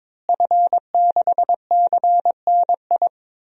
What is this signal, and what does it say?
Morse code, heard as a keyed tone, about 22 wpm, 700 Hz. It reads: F6CNI
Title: DXCC-100 Call Signs - 5 Characters - Repeated 2x - 22wpm